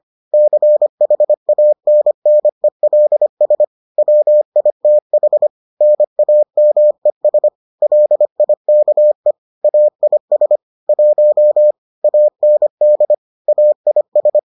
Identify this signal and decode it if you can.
Morse code, heard as a keyed tone, about 25 wpm, 600 Hz. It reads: CHANNELS WITH NAMES LIKE AIS 1 AND AIS